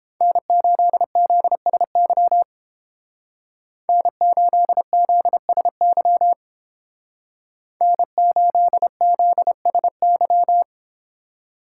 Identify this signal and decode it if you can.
Morse code, heard as a keyed tone, about 30 wpm, 700 Hz. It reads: N8ZSY N8ZSY N8ZSY